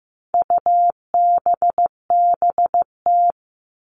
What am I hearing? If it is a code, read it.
Morse code, heard as a keyed tone, about 15 wpm, 700 Hz. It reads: UBBT
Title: All Cut Numbers - 4-Digit 15wpm